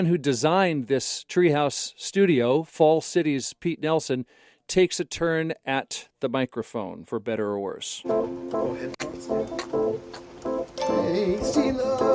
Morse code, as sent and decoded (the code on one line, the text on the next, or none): none